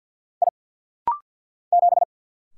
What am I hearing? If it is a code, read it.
Morse code, heard as a keyed tone, about 50 wpm, 700 Hz. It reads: I 7
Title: Single Letter-Number - Rapid-Fire 50wpm